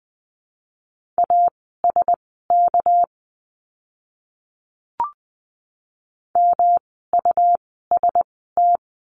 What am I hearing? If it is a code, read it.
Morse code, heard as a keyed tone, about 20 wpm, spaced as for 14 wpm, 700 Hz. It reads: ASK MUST
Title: 237 Lesson 38f - Letter q - Words 20wpm